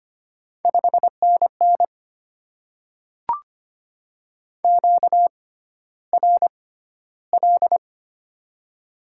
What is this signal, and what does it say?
Morse code, heard as a keyed tone, about 25 wpm, spaced as for 13 wpm, 700 Hz. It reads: 5NN QRL
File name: Top 20 QSO Elements - 6x Character Spacing - 25wpm